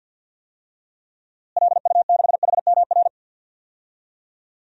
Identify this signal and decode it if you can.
Morse code, heard as a keyed tone, about 52 wpm, 700 Hz. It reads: LU6HMR
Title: DXCC-100 Call Signs - 6 Characters - Repeated 3x - 50wpm